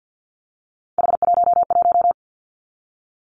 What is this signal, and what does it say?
Morse code, heard as a keyed tone, about 50 wpm, 700 Hz. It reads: H11
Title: Three Challenging Letter-Number Combinations 50wpm